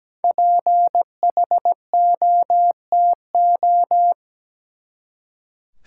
Morse code, heard as a keyed tone, about 17 wpm, 700 Hz. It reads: PHOTO
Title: Five Letter Words - 17wpm